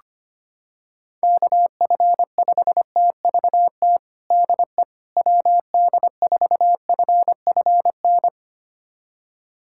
Morse code, heard as a keyed tone, about 25 wpm, 700 Hz. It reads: KF5TVT DE WD4FFN
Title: USA Callsign DE Callsign - 6 Characters - 25wpm